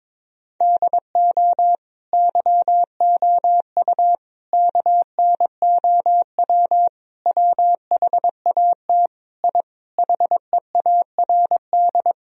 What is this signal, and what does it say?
Morse code, heard as a keyed tone, about 22 wpm, 700 Hz. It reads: DO YOU KNOW WHAT I HEARD
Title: Sentences from Top 1000 Words - Review Part 4 - 22wpm